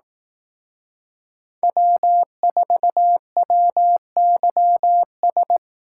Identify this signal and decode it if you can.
Morse code, heard as a keyed tone, about 18 wpm, 700 Hz. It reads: W4WYS
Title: Speed Racing - USA 1x3 Call Signs 15wpm